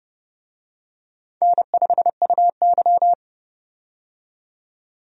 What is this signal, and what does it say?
Morse code, heard as a keyed tone, about 32 wpm, 700 Hz. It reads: N5UY
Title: USA Call Signs - 4 Characters - 30wpm